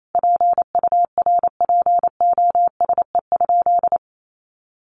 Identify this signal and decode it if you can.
Morse code, heard as a keyed tone, about 28 wpm, 700 Hz. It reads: PURPOSE?
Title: Sentences from Top 1200 Words 28wpm